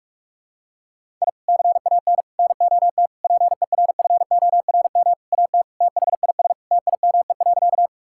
Morse code, heard as a keyed tone, about 46 wpm, 700 Hz. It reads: I CAN NOT PERFORM AT THIS TIME.
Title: Sentences from Top 1000 Words - Review Part 5 - 45wpm